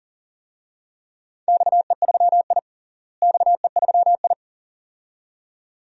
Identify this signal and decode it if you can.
Morse code, heard as a keyed tone, about 40 wpm, 700 Hz. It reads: XE3I XE3I
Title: DXCC-100 Call Signs - 4 Characters - Repeated 2x - 40wpm